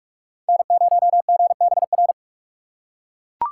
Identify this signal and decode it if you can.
Morse code, heard as a keyed tone, about 44 wpm, 700 Hz. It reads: N0GBR
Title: USA Call Signs - 6x Character Spacing - 45wpm